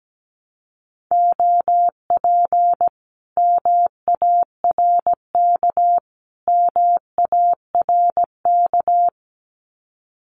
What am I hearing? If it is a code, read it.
Morse code, heard as a keyed tone, about 17 wpm, 700 Hz. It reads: OP MARK MARK
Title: Common QSO Phrases Encore 17wpm